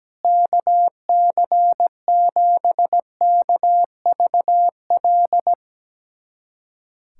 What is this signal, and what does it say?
Morse code, heard as a keyed tone, about 17 wpm, 700 Hz. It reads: KC7KVL
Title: USA Callsign DE Callsign - 6 Characters - 17wpm